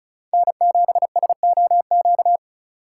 Morse code, heard as a keyed tone, about 34 wpm, 700 Hz. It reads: N7SOQ
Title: USA Call Signs - 5 Characters - 4x Character Spacing - 35wpm